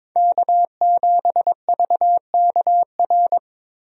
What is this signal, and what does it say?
Morse code, heard as a keyed tone, about 22 wpm, 700 Hz. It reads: K7VKR
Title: USA Call Signs - 5 Characters - Repeated 2x - 22wpm